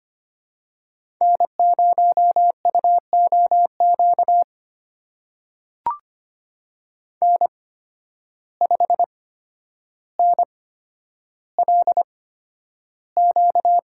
Morse code, heard as a keyed tone, about 25 wpm, spaced as for 11 wpm, 700 Hz. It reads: N0UOQ N5NLQ
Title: USA Call Signs - 8x Character Spacing - 25wpm